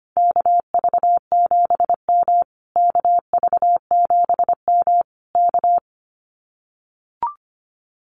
Morse code, heard as a keyed tone, about 25 wpm, 700 Hz. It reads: KV7M KV7M K
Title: Common QSO Phrases Encore 25wpm